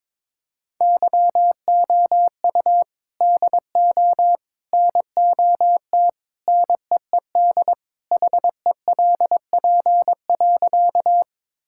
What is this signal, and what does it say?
Morse code, heard as a keyed tone, about 21 wpm, 700 Hz. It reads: YOU DO NOT NEED HELP.
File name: practice / 22wpm / Sentences with 3-Word Phrases - 22wpm